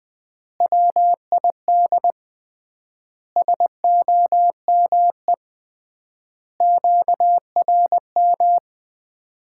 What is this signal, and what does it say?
Morse code, heard as a keyed tone, about 21 wpm, 700 Hz. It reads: WID SOME QRM
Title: Common QSO Phrases - 3x Word-Spacing - 20wpm